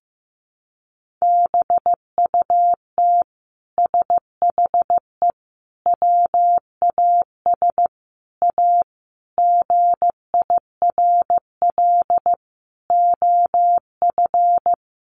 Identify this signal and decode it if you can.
Morse code, heard as a keyed tone, about 14 wpm, 700 Hz. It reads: BUT SHE WAS A GIRL OF
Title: Sentences from Top 1400 Words 15wpm